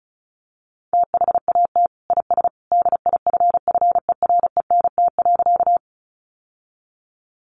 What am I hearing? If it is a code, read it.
Morse code, heard as a keyed tone, about 34 wpm, 700 Hz. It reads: THAT IS DIFFERENT.
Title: Sentences from Top 600 Words 35wpm